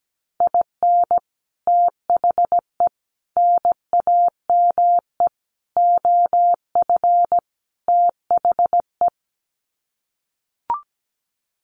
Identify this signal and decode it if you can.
Morse code, heard as a keyed tone, about 17 wpm, 700 Hz. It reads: IN THE NAME OF THE
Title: Sets of 5 Words Encore 17wpm